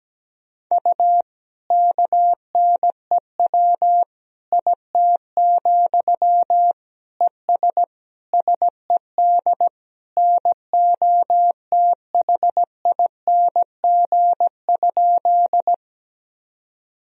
Morse code, heard as a keyed tone, about 17 wpm, 700 Hz. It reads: U KNEW IT, ES SED NOTHING?
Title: Sentences from Top 500 Words and Abbreviations 17wpm